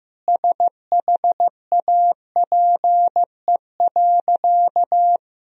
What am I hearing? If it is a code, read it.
Morse code, heard as a keyed tone, about 15 wpm, 700 Hz. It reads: SHAPE.
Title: Sentences from Top 1000 Words - Review Part 3 - 15wpm